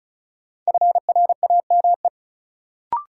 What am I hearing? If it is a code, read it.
Morse code, heard as a keyed tone, about 36 wpm, 700 Hz. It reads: FRAME
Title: Sentences from Top 1000 Words - Review Part 6 - New Word Drill - 35wpm